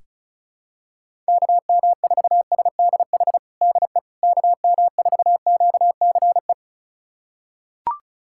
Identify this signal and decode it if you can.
Morse code, heard as a keyed tone, about 36 wpm, 700 Hz. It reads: KM4SDH DE KM4QCE
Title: USA Callsign DE Callsign 35wpm